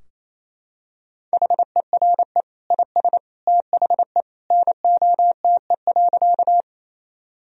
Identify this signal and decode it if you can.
Morse code, heard as a keyed tone, about 28 wpm, 700 Hz. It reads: HERE IS THE NOTE.
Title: Sentences from Top 1000 Words - Review Part 2 - 28wpm